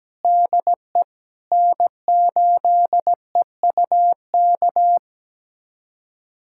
Morse code, heard as a keyed tone, about 17 wpm, 700 Hz. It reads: DE N8EUK
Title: USA Callsign DE Callsign - 5 Characters - 17wpm